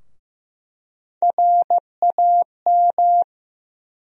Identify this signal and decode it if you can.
Morse code, heard as a keyed tone, about 15 wpm, 700 Hz. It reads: RAM
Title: USA Call Sign Postfixes - 3 Letters 15wpm